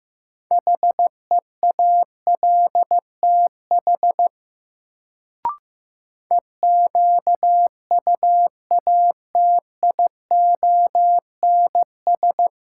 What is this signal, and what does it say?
Morse code, heard as a keyed tone, about 15 wpm, 700 Hz. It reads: HEALTH EQUATIONS
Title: Sets of 5 Words - New Word Drill - 15wpm